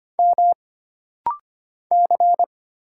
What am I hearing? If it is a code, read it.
Morse code, heard as a keyed tone, about 25 wpm, 700 Hz. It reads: M C
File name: Single Character (Letter:Number:Punctuation:Prosign) - Rapid-Fire - 25wpm